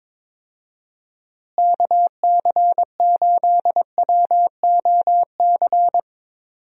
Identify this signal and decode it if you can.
Morse code, heard as a keyed tone, about 22 wpm, 700 Hz. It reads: KC8WOC
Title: USA Call Signs - 6 Characters - Repeated 2x - 22wpm